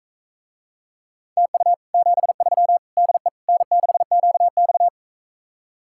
Morse code, heard as a keyed tone, about 44 wpm, 700 Hz. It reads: TU 73 DE N6QX